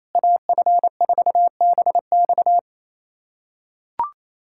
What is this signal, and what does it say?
Morse code, heard as a keyed tone, about 28 wpm, 700 Hz. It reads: AF4BX
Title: USA Callsign DE Callsign - 5 Characters - 28wpm